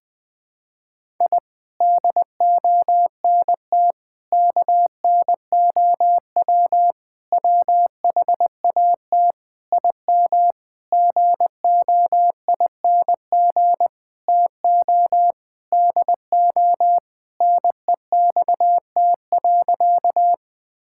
Morse code, heard as a keyed tone, about 20 wpm, 700 Hz. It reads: I DONT KNOW WHAT IM GOING TO DO NEXT.
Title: Sentences with 8-Word Phrases - Spoken Beginning - 20wpm